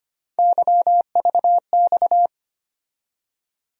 Morse code, heard as a keyed tone, about 25 wpm, 700 Hz. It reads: YVX
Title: USA Call Sign Postfixes - 3 Letters 25wpm